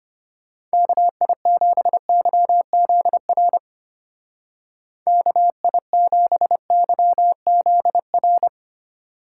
Morse code, heard as a keyed tone, about 30 wpm, 700 Hz. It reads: KI7YZR KI7YZR
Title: Speed Racing - USA 2x3 Call Signs 25wpm